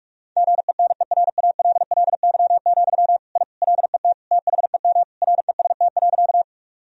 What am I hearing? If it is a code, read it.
Morse code, heard as a keyed tone, about 46 wpm, 700 Hz. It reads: GENERALLY, I LET THEM REST.
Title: Sentences from Top 1000 Words - Review Part 4 - 45wpm